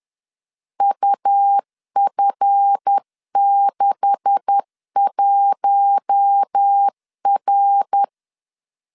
Morse code, heard as a keyed tone, about 10 wpm, 800 Hz. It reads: UF61R
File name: LEZIONI / 9 286 10wpm 2735